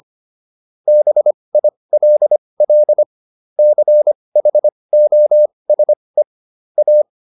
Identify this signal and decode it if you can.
Morse code, heard as a keyed tone, about 25 wpm, spaced as for 20 wpm, 600 Hz. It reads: BILL CHOSE A